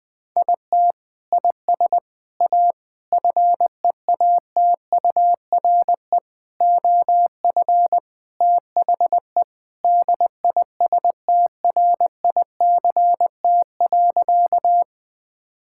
Morse code, heard as a keyed tone, about 20 wpm, 700 Hz. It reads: IT IS A FEATURE OF THE DISTRICT.
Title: Sentences from Top 1600 Words 20wpm